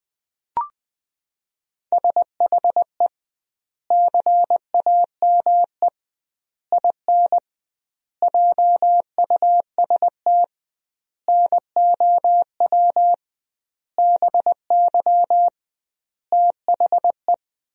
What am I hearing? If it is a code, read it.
Morse code, heard as a keyed tone, about 21 wpm, 700 Hz. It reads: SHE CAME IN JUST NOW BY THE
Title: 2x Word-Spacing - Sentences from Top 300 Words - 20wpm